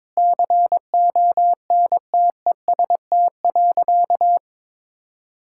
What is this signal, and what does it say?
Morse code, heard as a keyed tone, about 22 wpm, 700 Hz. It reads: CONTEST.
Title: Sentences from Top 1600 Words 22wpm